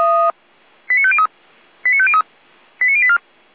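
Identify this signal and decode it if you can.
Five-tone selective call (ZVEI), heard as quick runs of five tones, tones of about 70 ms. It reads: 89503 89503 89084
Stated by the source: SDR-received ZVEI five-tone selective call, code 89120.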